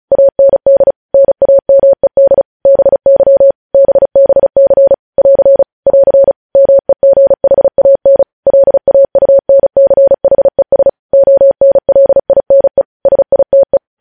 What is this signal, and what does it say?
Morse code, heard as a keyed tone, about 36 wpm, 550 Hz. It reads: AND NAMED BY BBC <AR> <AR> MEGHAN LAUNCHES ONLINE SITE
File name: podcast / News headlines-1x-35x35